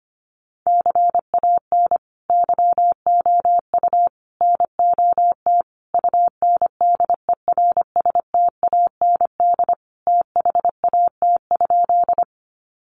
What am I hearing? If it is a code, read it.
Morse code, heard as a keyed tone, about 25 wpm, 700 Hz. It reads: CAN YOU NOT UNDERSTAND THAT?
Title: Sentences from Top 600 Words 25wpm